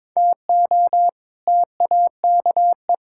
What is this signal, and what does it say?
Morse code, heard as a keyed tone, about 22 wpm, 700 Hz. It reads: TO TAKE
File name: practice / 22wpm / Build A Phrase - 5-Word Phrases - 22wpm